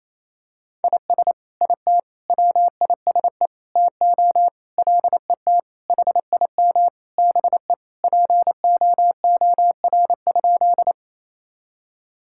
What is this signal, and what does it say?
Morse code, heard as a keyed tone, about 28 wpm, 700 Hz. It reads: IS IT WISE TO LET HIM BE POOR?
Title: Sentences from Top 1600 Words 28wpm